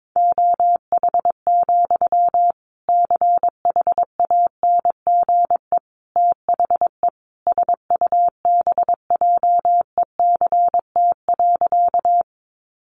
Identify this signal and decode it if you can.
Morse code, heard as a keyed tone, about 22 wpm, 700 Hz. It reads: OH, CHANGE THE SUBJECT.
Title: Sentences from Top 800 Words 22wpm